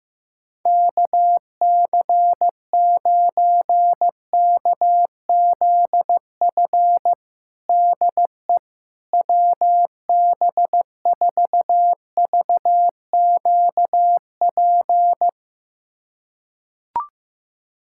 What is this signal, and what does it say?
Morse code, heard as a keyed tone, about 15 wpm, 700 Hz. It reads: KC9KZF DE WB4VQP
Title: USA Callsign DE Callsign - 6 Characters - 15wpm